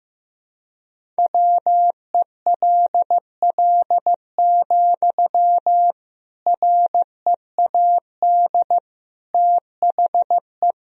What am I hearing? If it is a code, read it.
Morse code, heard as a keyed tone, about 15 wpm, 700 Hz. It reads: WELL, READ THE
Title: Sentences from Top 300 Words and Abbreviations 15wpm